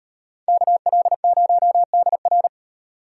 Morse code, heard as a keyed tone, about 33 wpm, 700 Hz. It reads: KL0DR